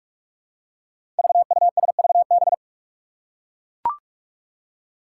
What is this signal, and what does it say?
Morse code, heard as a keyed tone, about 42 wpm, 700 Hz. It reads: VUSVB
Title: Five Challenging Letter-Number Combinations 45wpm